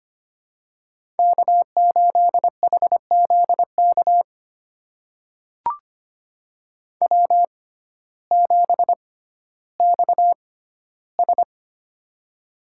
Morse code, heard as a keyed tone, about 25 wpm, spaced as for 14 wpm, 700 Hz. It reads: K8HZK W7XS
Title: USA Call Signs - 6x Character Spacing - 25wpm